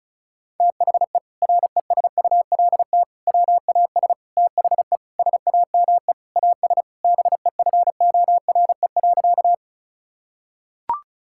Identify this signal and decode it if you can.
Morse code, heard as a keyed tone, about 34 wpm, 700 Hz. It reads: THE RESULT WAS THE SAME AS BEFORE.
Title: Sentences from Top 1000 Words - Review Part 5 - 35wpm